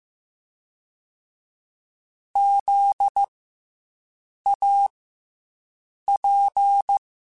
Morse code, heard as a keyed tone, about 15 wpm, 800 Hz. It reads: Z A P